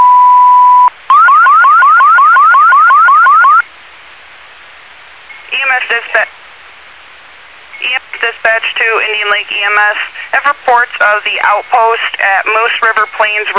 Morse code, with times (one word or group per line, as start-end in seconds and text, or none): none